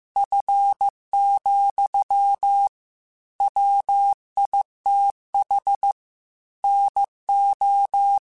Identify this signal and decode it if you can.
Morse code, heard as a keyed tone, about 15 wpm, 800 Hz. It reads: F, WITH NO
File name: Weather Sets / Weather Sets-mp3s / Weather 3 17 @13